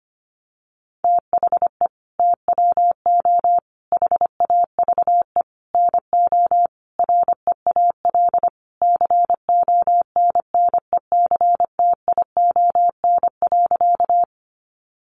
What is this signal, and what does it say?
Morse code, heard as a keyed tone, about 25 wpm, 700 Hz. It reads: THE TWO HAVE NO REAL CONNECTION.
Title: Sentences from Top 1300 Words 25wpm